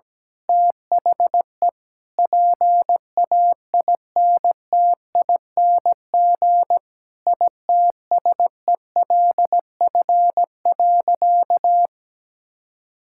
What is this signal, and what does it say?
Morse code, heard as a keyed tone, about 17 wpm, 700 Hz. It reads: THE PAINTING ITSELF.
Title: Sentences from Top 1000 Words - Review Part 6 - 17wpm